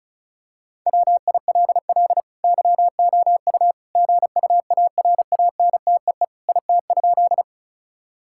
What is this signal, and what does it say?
Morse code, heard as a keyed tone, about 34 wpm, 700 Hz. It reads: WILL YOU GUARANTEE IT?
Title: Sentences from Top 1600 Words 35wpm